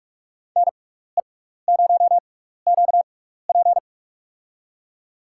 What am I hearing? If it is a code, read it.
Morse code, heard as a keyed tone, about 46 wpm, spaced as for 16 wpm, 700 Hz. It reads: NE0QP
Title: USA Call Signs - 5 Characters - 6x Character Spacing - 45wpm